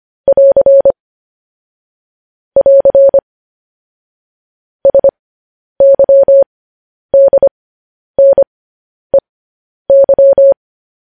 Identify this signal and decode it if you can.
Morse code, heard as a keyed tone, about 25 wpm, spaced as for 10 wpm, 550 Hz. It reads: <AR> <AR> SYDNEY